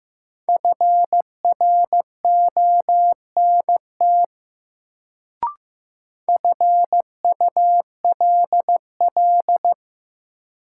Morse code, heard as a keyed tone, about 15 wpm, 700 Hz. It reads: FRONT FULL
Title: Sentences from Top 500 Words - New Word Drill - 15wpm